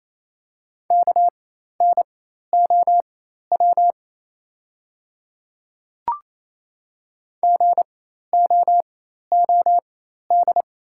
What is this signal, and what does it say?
Morse code, heard as a keyed tone, about 28 wpm, spaced as for 13 wpm, 700 Hz. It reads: KNOW GOOD
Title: 4x Character Spacing - Top 200 Words - 28wpm